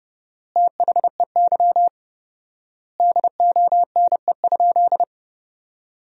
Morse code, heard as a keyed tone, about 30 wpm, 700 Hz. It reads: THEY DONE?
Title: 4x Word-Spacing - Sentences from Top 500 Words - 30wpm